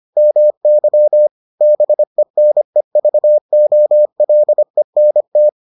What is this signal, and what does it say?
Morse code, heard as a keyed tone, about 25 wpm, 600 Hz. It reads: MY BENEVOLENT